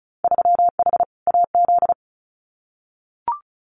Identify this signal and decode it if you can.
Morse code, heard as a keyed tone, about 34 wpm, 700 Hz. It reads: 3H AZ
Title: Winter Field Day Exchanges Encore 35wpm